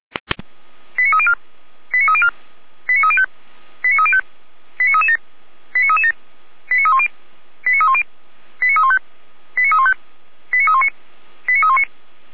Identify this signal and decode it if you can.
Five-tone selective call (ZVEI), heard as quick runs of five tones, tones of about 70 ms. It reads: 89304 89304 89305 89305 89337 89337 89310 89310 89315 89315 89319 89319